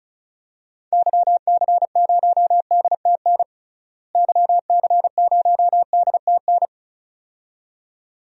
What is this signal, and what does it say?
Morse code, heard as a keyed tone, about 36 wpm, 700 Hz. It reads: YC0DTN YC0DTN
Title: DXCC-100 Call Signs - 6 Characters - Repeated 2x - 35wpm